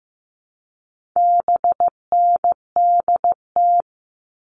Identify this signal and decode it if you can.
Morse code, heard as a keyed tone, about 15 wpm, 700 Hz. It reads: BNDT
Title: All Cut Numbers - 4-Digit 15wpm